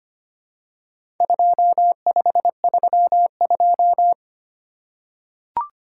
Morse code, heard as a keyed tone, about 25 wpm, 700 Hz. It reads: 2532
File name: practice / 25wpm / Four Letter-Number Combinations - 25wpm